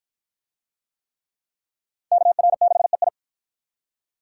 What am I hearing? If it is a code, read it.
Morse code, heard as a keyed tone, about 53 wpm, 700 Hz. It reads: KR6EI